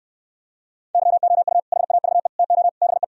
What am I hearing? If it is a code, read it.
Morse code, heard as a keyed tone, about 67 wpm, 700 Hz. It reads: YOU HAVE TO BE